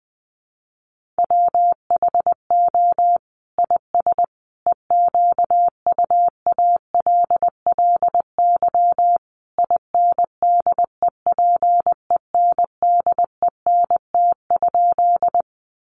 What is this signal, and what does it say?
Morse code, heard as a keyed tone, about 19 wpm, 700 Hz. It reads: WHO IS EQUALLY INDEPENDENT?
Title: Sentences from Top 1400 Words 20wpm